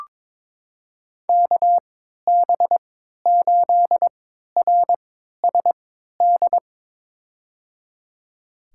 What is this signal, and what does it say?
Morse code, heard as a keyed tone, about 22 wpm, spaced as for 12 wpm, 700 Hz. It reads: KB8RSD
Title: USA Call Signs - 6 Characters - 3x Character Spacing - 22wpm